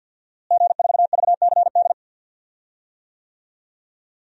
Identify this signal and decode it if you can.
Morse code, heard as a keyed tone, about 46 wpm, 700 Hz. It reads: G4VCD
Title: DXCC-100 Call Signs - 5 Characters - Repeated 2x - 50wpm